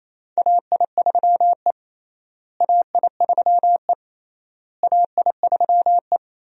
Morse code, heard as a keyed tone, about 28 wpm, 700 Hz. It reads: AI3E AI3E AI3E
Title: USA Call Signs - 4 Characters - Repeated 3x - 28wpm